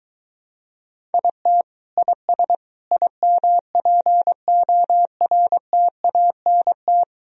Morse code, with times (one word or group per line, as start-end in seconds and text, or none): 1.1-1.6 IT
2.0-2.6 IS
2.9-7.0 IMPORTANT